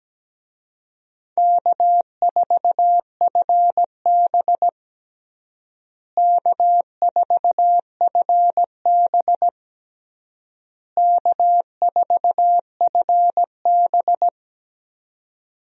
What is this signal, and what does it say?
Morse code, heard as a keyed tone, about 17 wpm, 700 Hz. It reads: K4FB K4FB K4FB
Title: USA Call Signs - 4 Characters - Repeated 3x - 17wpm